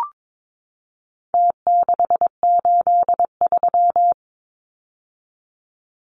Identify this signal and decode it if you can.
Morse code, heard as a keyed tone, about 22 wpm, 700 Hz. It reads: T683
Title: Numbers - Contest Style - 4-Digit 22wpm